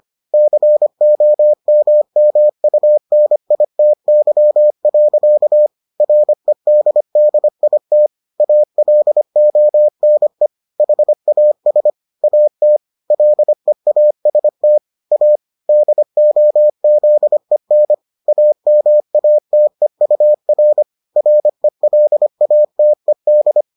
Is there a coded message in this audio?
Yes: Morse code from 0.3 to 23.6 s, heard as a keyed tone, about 25 wpm, 600 Hz. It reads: COMMUNITY. REDDIT ALONE HAS AT LEAST A DOZEN AMATEUR RELATED